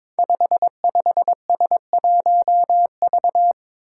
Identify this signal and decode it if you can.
Morse code, heard as a keyed tone, about 22 wpm, 700 Hz. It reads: 55S1V
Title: Five Challenging Letter-Number Combinations 22wpm